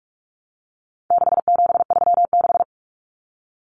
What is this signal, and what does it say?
Morse code, heard as a keyed tone, about 44 wpm, 700 Hz. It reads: 6736
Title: Common Cut Numbers - 4-Digit 45wpm